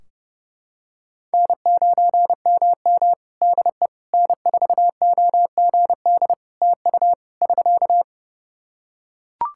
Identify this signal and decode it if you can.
Morse code, heard as a keyed tone, about 30 wpm, 700 Hz. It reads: N9MM DE N4OGD TU <SK>
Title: K1USN SST QSO Practice 30wpm